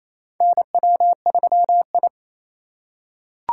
Morse code, heard as a keyed tone, about 27 wpm, 700 Hz. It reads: NW3I
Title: USA Call Signs - 4 Characters - 4x Character Spacing - 28wpm